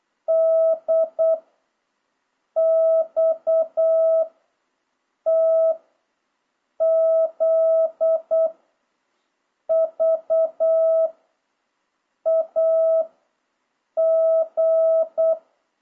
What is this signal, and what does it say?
Morse code, heard as a keyed tone, about 8 wpm, 660 Hz. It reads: D X T Z V A G
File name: jh8cbh / 2026cw